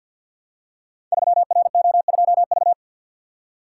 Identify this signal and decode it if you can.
Morse code, heard as a keyed tone, about 50 wpm, 700 Hz. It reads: 3RO2V
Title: Code Groups - 5 Letter:Numbers - 50wpm